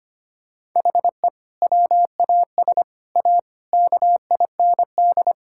Morse code, heard as a keyed tone, about 25 wpm, 700 Hz. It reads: HE WAS A KIND
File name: practice / 25wpm / Sentences with 3-Word Phrases - 25wpm